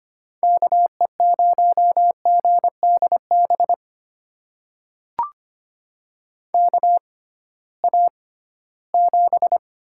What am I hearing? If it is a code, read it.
Morse code, heard as a keyed tone, about 25 wpm, spaced as for 16 wpm, 700 Hz. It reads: KE0GDB KA7
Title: USA Call Signs - 6 Characters - 6x Character Spacing - 25wpm